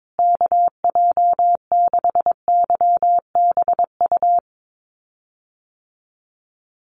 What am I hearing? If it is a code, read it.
Morse code, heard as a keyed tone, about 22 wpm, 700 Hz. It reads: KJ6YBU
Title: USA Call Signs Repeated 2x 22wpm